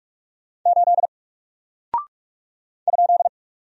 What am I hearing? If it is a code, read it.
Morse code, heard as a keyed tone, about 44 wpm, 700 Hz. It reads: 8 ?
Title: Single Character (Letter:Number:Punctuation:Prosign) Alt 45wpm